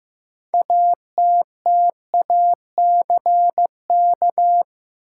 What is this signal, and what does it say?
Morse code, heard as a keyed tone, about 15 wpm, 700 Hz. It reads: ATTACK
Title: Sentences from Top 1500 Words - New Word Drill - 15wpm